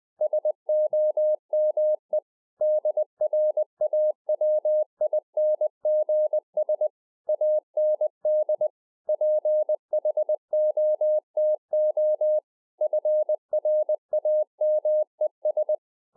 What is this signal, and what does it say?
Morse code, heard as a keyed tone, about 20 wpm, 610 Hz. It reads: SOME DRAWINGS AND PHOTO FRAMES